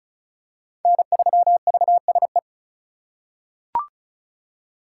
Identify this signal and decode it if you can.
Morse code, heard as a keyed tone, about 34 wpm, 700 Hz. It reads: N3VSE
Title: USA Call Signs - 5 Characters - 4x Character Spacing - 35wpm